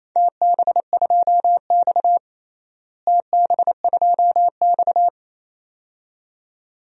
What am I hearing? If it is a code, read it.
Morse code, heard as a keyed tone, about 28 wpm, 700 Hz. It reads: TB2X TB2X
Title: DXCC-100 Call Signs - 4 Characters - Repeated 2x - 28wpm